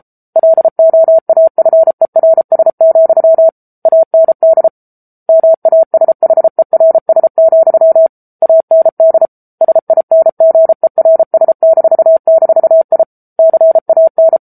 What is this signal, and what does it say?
Morse code, heard as a keyed tone, about 34 wpm, 650 Hz. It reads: LOAFERS, AND MASHERS, AND SINGERS--I CAN
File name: Book Two / 07-Chapter 7 - 33 WPM